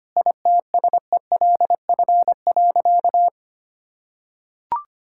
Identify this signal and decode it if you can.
Morse code, heard as a keyed tone, about 25 wpm, 700 Hz. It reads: ITSELF.
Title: Sentences from Top 1000 Words - Review Part 5 - 25wpm